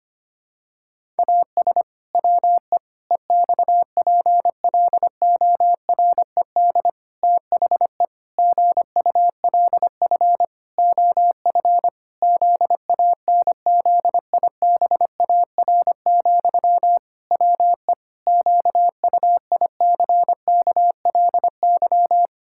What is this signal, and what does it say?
Morse code, heard as a keyed tone, about 25 wpm, 700 Hz. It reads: AS WE EXPLORED THE GULF OF ZANZIBAR, WE QUICKLY
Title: Pangram Challenge v2 - 25wpm